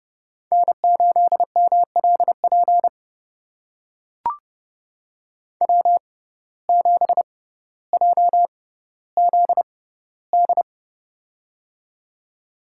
Morse code, heard as a keyed tone, about 31 wpm, spaced as for 15 wpm, 700 Hz. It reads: N8MLP W7JZD
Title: USA Call Signs - 5 Characters - 6x Character Spacing - 30wpm